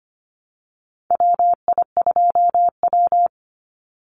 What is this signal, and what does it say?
Morse code, heard as a keyed tone, about 25 wpm, 700 Hz. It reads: WI2W
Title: NA SOTA Chaser Callsigns 25wpm